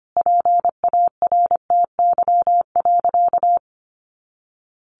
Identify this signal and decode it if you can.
Morse code, heard as a keyed tone, about 25 wpm, 700 Hz. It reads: PARTY.
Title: Sentences from Top 1500 Words 25wpm